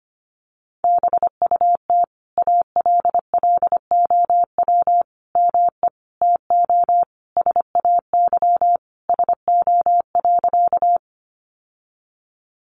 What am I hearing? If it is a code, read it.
Morse code, heard as a keyed tone, about 25 wpm, 700 Hz. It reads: BUT ALLOW ME TO SAY SO.
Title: Sentences from Top 700 Words 25wpm